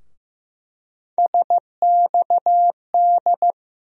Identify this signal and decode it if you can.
Morse code, heard as a keyed tone, about 15 wpm, 700 Hz. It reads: SXD
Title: USA Call Sign Postfixes - 3 Letters 15wpm